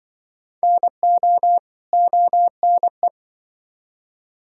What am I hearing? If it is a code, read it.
Morse code, heard as a keyed tone, about 24 wpm, 700 Hz. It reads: NO ONE
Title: Speed-Racing - Sets of 2 Words 20wpm